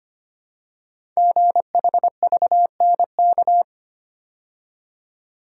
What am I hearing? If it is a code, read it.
Morse code, heard as a keyed tone, about 25 wpm, 700 Hz. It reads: GHVNK
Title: Code Groups - 5 Letters - 25wpm